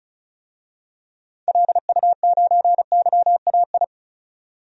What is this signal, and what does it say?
Morse code, heard as a keyed tone, about 36 wpm, 700 Hz. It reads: LU9YAI
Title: DXCC-100 Call Signs - 6 Characters - 35wpm